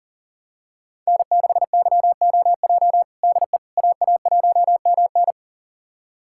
Morse code, heard as a keyed tone, about 40 wpm, 700 Hz. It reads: N6YOJ DE AA1MN
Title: USA Callsign DE Callsign - 5 Characters - 40wpm